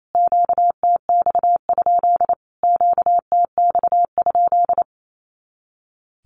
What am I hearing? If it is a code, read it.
Morse code, heard as a keyed tone, about 28 wpm, 700 Hz. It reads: QTX? QTX?
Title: Q-Codes Repeated 3x 28wpm